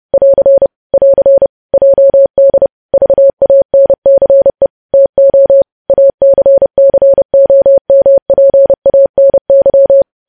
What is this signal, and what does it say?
Morse code, heard as a keyed tone, about 30 wpm, 550 Hz. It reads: <AR> <AR> JD VANCE TO ACCOMPANY
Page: news headlines in morse code at 30 wpm